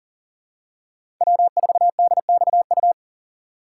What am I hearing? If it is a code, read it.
Morse code, heard as a keyed tone, about 40 wpm, 700 Hz. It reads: W4DXU
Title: USA Call Signs - 5 Characters - 2x Character Spacing - 40wpm